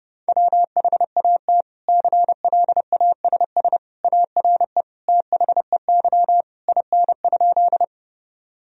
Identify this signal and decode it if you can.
Morse code, heard as a keyed tone, about 30 wpm, 700 Hz. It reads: WHAT CLASS ARE THEY IN?
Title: Sentences from Top 1000 Words - Review Part 2 - 30wpm